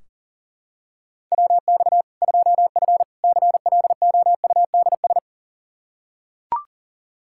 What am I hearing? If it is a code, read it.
Morse code, heard as a keyed tone, about 40 wpm, 700 Hz. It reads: WX 2F CLOUDS
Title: Practice QSO - Exchange 2 - Info - 40wpm